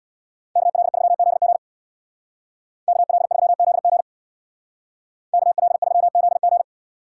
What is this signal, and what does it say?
Morse code, heard as a keyed tone, about 65 wpm, 690 Hz. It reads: KL3ZG KL3ZG KL3ZG